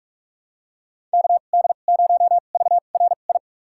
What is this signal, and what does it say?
Morse code, heard as a keyed tone, about 47 wpm, spaced as for 33 wpm, 700 Hz. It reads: KD0VRI
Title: USA Call Signs - 6 Characters - 2x Character Spacing - 45wpm